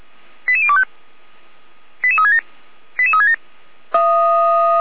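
Five-tone selective call (ZVEI), heard as quick runs of five tones, tones of about 70 ms. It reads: 88025 88367 88367
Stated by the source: ZVEI five-tone selective call received by SDR.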